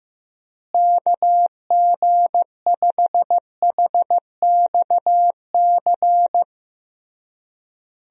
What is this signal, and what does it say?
Morse code, heard as a keyed tone, about 16 wpm, 700 Hz. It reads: KG5HXC
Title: USA Call Signs - 6 Characters - Repeated 3x - 15wpm